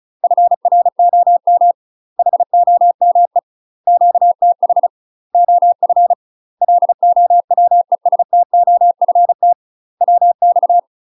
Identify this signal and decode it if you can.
Morse code, heard as a keyed tone, about 36 wpm, 700 Hz. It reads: FROM HOME QTH OF LOWESTOFT WX
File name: UK MASTER QSOs TRAINING / 35WPM / UK MASTER QSOs 0043